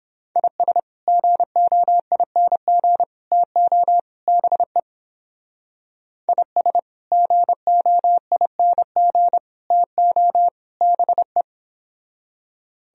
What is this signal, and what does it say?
Morse code, heard as a keyed tone, about 31 wpm, 700 Hz. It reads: IS GOING TO BE IS GOING TO BE